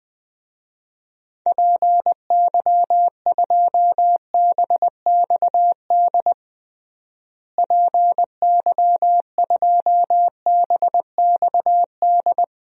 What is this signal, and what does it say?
Morse code, heard as a keyed tone, about 20 wpm, 700 Hz. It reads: PY2BXD PY2BXD
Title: DXCC-100 Call Signs - 6 Characters - Repeated 2x - 20wpm